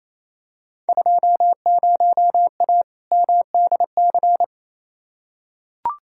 Morse code, heard as a keyed tone, about 28 wpm, 700 Hz. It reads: 20A MDC
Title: Field Day Exchanges Encore - 28wpm